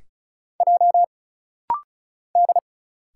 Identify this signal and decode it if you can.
Morse code, heard as a keyed tone, about 36 wpm, 700 Hz. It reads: J D
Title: Single Letters - Rapid-Fire 35wpm